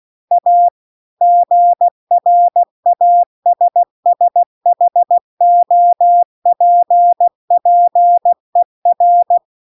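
Morse code, heard as a keyed tone, about 16 wpm, 700 Hz. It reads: A GRASSHOPPER
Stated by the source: Morse practice text at 16 WPM.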